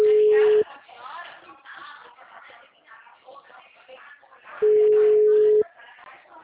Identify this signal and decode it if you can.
Ringback tone (Europe, 425 Hz), with speech.
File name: out- -2010- - - .5743